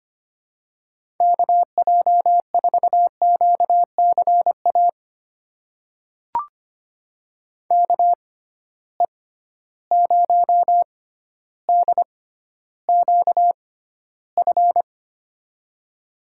Morse code, heard as a keyed tone, about 25 wpm, spaced as for 13 wpm, 700 Hz. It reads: KJ4QCA KE0DQF
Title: USA Call Signs - 6 Characters - 6x Character Spacing - 25wpm